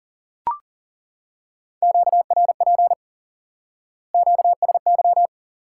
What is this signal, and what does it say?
Morse code, heard as a keyed tone, about 40 wpm, 700 Hz. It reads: QRP QSY
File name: IQE Course 3 - Lesson 89 - YAGI - Two Elements - 40wpm